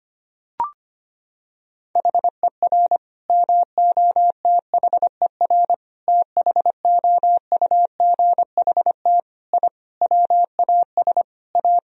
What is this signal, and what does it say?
Morse code, heard as a keyed tone, about 25 wpm, 700 Hz. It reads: HER MOTHER THOUGHT I WAS A
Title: Sentences from Top 1000 Words - Review Part 4 - 25wpm